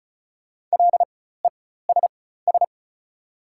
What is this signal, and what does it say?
Morse code, heard as a keyed tone, about 34 wpm, spaced as for 16 wpm, 690 Hz. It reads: LESS
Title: 4x Character Spacing - Top 500 Words - 35wpm